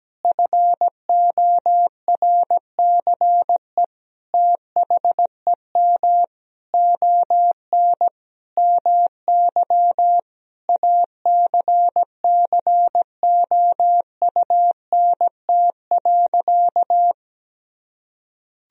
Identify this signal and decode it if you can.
Morse code, heard as a keyed tone, about 17 wpm, 700 Hz. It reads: FORCE THEM ON MY ACCOUNT.
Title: Sentences from Top 1000 Words - Review Part 3 - 17wpm